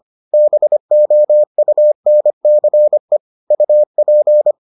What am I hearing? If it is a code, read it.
Morse code, heard as a keyed tone, about 25 wpm, 600 Hz. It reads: BOUNCE UP